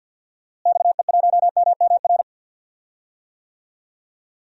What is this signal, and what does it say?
Morse code, heard as a keyed tone, about 47 wpm, 700 Hz. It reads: XE1MMR
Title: DXCC-100 Call Signs - 6 Characters - Repeated 3x - 50wpm